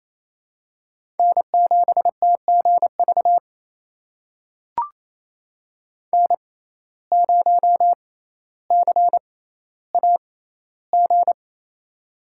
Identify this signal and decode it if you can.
Morse code, heard as a keyed tone, about 28 wpm, spaced as for 14 wpm, 700 Hz. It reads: N7TGV N0CAG
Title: USA Call Signs - 5 Characters - 6x Character Spacing - 28wpm